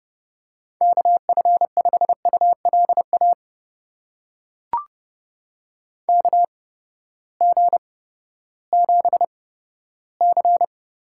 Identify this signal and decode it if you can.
Morse code, heard as a keyed tone, about 30 wpm, spaced as for 15 wpm, 700 Hz. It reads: KF5ULA KG7C
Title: USA Call Signs - 6 Characters - 8x Character Spacing - 30wpm